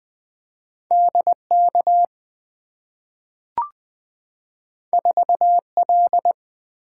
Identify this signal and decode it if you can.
Morse code, heard as a keyed tone, about 20 wpm, 700 Hz. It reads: DK 4L
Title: Two Letter-Number Combinations - 20wpm